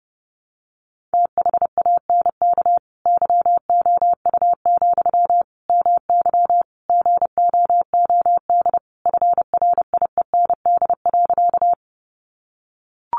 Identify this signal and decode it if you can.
Morse code, heard as a keyed tone, about 30 wpm, 700 Hz. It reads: THANK YOU, MY GOOD FRIEND.
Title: Sentences from Top 600 Words 30wpm